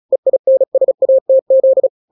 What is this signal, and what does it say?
Morse code, heard as a keyed tone, about 36 wpm, 530 Hz. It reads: EINSATZ
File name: daily / 35wpm